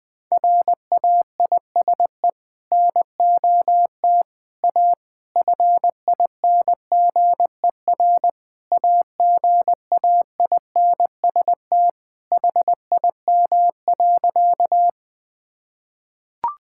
Morse code, heard as a keyed tone, about 20 wpm, 700 Hz. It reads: RAISE NOT A FINGER AGAINST HIM.
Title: Sentences from Top 1600 Words 20wpm